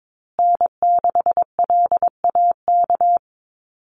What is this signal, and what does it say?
Morse code, heard as a keyed tone, about 22 wpm, 700 Hz. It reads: N6LAK
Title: USA Call Signs Repeated 2x 22wpm